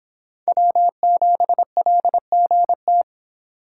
Morse code, heard as a keyed tone, about 29 wpm, 700 Hz. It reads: W7LGT